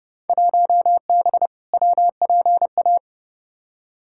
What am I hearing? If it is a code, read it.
Morse code, heard as a keyed tone, about 31 wpm, 700 Hz. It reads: 1B WPA
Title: Field Day Exchanges - 30wpm